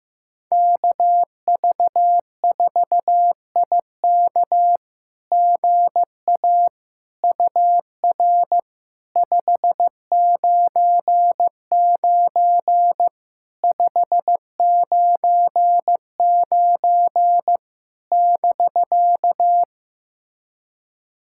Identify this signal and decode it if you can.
Morse code, heard as a keyed tone, about 15 wpm, 700 Hz. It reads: KV4IK GA UR 599 599 <BK>
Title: SOTA QSO Practice 15wpm